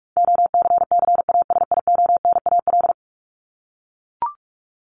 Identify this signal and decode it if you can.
Morse code, heard as a keyed tone, about 44 wpm, 700 Hz. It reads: OCCASIONAL
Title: LY Suffix - Top 100 Words with Base Words 45wpm